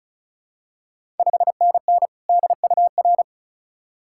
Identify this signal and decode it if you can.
Morse code, heard as a keyed tone, about 35 wpm, 690 Hz. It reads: 5NN DUR
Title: Ontario QSO Party Exchange 35wpm